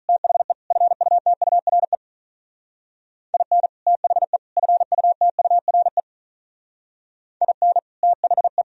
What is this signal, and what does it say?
Morse code, heard as a keyed tone, about 40 wpm, 700 Hz. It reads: THE FUTURE IN THE FUTURE IN THE
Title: Speed-Racing - Sets of 3 Words 35wpm